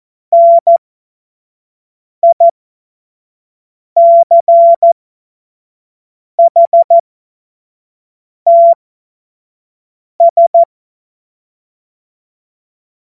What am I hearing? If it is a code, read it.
Morse code, heard as a keyed tone, about 14 wpm, spaced as for 5 wpm, 680 Hz. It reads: NICHTS
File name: Witz-09-September-2025-25-F70